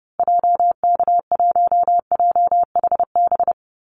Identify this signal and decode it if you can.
Morse code, heard as a keyed tone, about 30 wpm, 700 Hz. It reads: JK1JHB
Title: Call Signs Repeated 2x - 30wpm